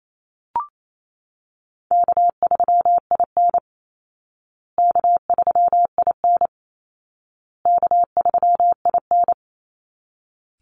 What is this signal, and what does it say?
Morse code, heard as a keyed tone, about 28 wpm, 700 Hz. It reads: K3IN K3IN K3IN
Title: Easy Call Signs Repeated 3x - 28wpm